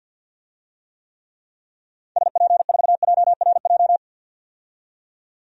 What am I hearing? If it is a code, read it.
Morse code, heard as a keyed tone, about 52 wpm, 700 Hz. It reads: SP4JRJ
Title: DXCC-100 Call Signs - 6 Characters - 50wpm